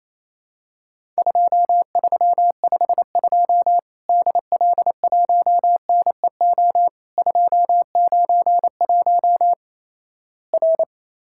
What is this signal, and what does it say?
Morse code, heard as a keyed tone, about 28 wpm, 700 Hz. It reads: 2352 DL1NEO 291 R
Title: Worked All Europe DX Contest - QTC Exchange - 28wpm